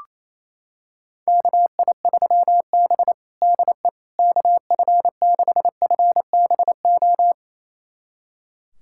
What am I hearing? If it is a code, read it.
Morse code, heard as a keyed tone, about 28 wpm, 700 Hz. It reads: KI3B DE KF6FBO
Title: USA Callsign DE Callsign 28wpm